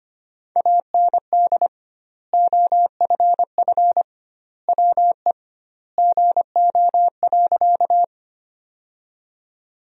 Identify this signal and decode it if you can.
Morse code, heard as a keyed tone, about 25 wpm, 700 Hz. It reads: AND OFF WE GO.
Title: 2x Word-Spacing - Sentences from Top 200 Words - 25wpm